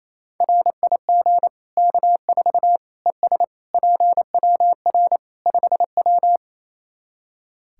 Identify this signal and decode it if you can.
Morse code, heard as a keyed tone, about 28 wpm, 700 Hz. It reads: RIG K4 ES PWR 5W
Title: K1USN SST QSO Practice 28wpm